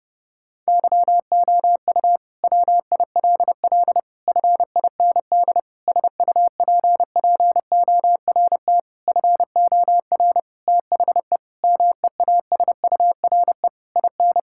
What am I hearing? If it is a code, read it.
Morse code, heard as a keyed tone, about 32 wpm, 700 Hz. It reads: YOU WILL FIND SUPPORT FOR THE MEASURE IN
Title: Sentences from Top 1000 Words - Review Part 5 - 30wpm